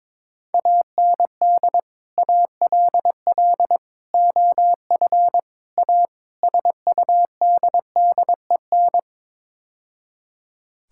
Morse code, heard as a keyed tone, about 22 wpm, 700 Hz. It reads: AND ALL OF A SUDDEN
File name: practice / 22wpm / Build A Phrase - 5-Word Phrases - 22wpm